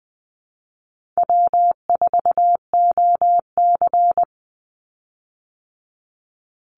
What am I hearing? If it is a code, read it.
Morse code, heard as a keyed tone, about 20 wpm, 700 Hz. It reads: W4OC
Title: Easy Call Signs - 20wpm